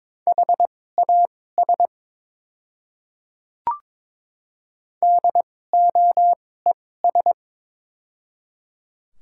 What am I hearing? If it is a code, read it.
Morse code, heard as a keyed tone, about 22 wpm, spaced as for 16 wpm, 700 Hz. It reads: HAS DOES
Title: 2x Character Spacing - Top 500 Words - 22wpm